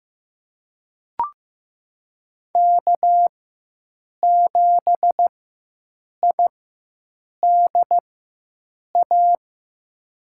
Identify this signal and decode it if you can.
Morse code, heard as a keyed tone, about 15 wpm, spaced as for 7 wpm, 700 Hz. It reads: K7IDA
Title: USA Call Signs - 5 Characters - 4x Character Spacing - 15wpm